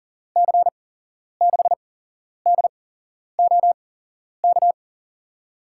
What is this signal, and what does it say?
Morse code, heard as a keyed tone, about 41 wpm, spaced as for 11 wpm, 700 Hz. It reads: C6DOK
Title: USA Call Signs - 6 Characters - 8x Character Spacing - 40wpm